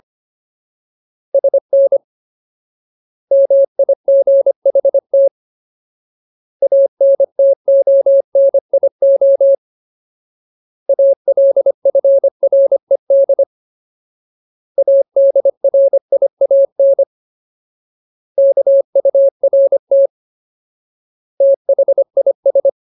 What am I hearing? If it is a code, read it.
Morse code, heard as a keyed tone, about 25 wpm, 550 Hz. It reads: SN MIGHT ANTONIO ALFRED ADRIAN KURT THIS
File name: Ultimate Word List / 25WPM Ultimate